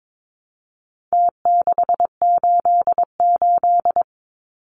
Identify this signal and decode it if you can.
Morse code, heard as a keyed tone, about 22 wpm, 700 Hz. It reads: T688
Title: Numbers - Contest Style - 4-Digit 22wpm